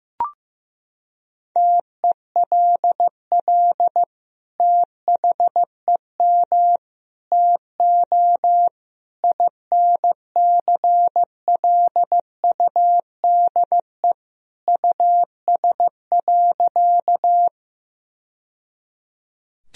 Morse code, heard as a keyed tone, about 15 wpm, 700 Hz. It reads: TELL THEM TO INCLUDE US.
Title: Sentences from Top 1000 Words - Review Part 3 - 15wpm